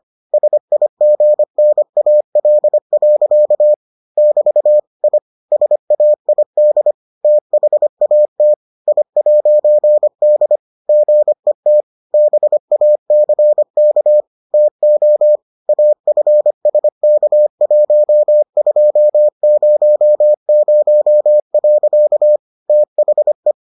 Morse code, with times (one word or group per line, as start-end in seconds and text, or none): 0.3-3.7 SIGNAL.
4.2-4.8 <BT>
5.0-5.2 I
5.5-6.9 SAID
7.2-8.5 THAT
8.9-10.6 I'D
10.9-11.8 GET
12.1-14.2 BACK
14.5-15.4 TO
15.7-22.4 AFSK1200.
22.7-23.5 THE